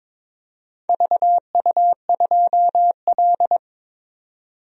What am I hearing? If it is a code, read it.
Morse code, heard as a keyed tone, about 22 wpm, 700 Hz. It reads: VU2L
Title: DXCC-100 Call Signs - 4 Characters - 22wpm